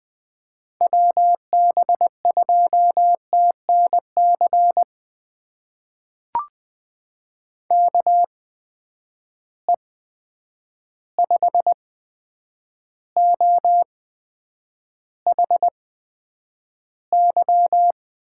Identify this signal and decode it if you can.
Morse code, heard as a keyed tone, about 20 wpm, spaced as for 9 wpm, 700 Hz. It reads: WB2TNC KE5OHY
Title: USA Call Signs - 6 Characters - 8x Character Spacing - 20wpm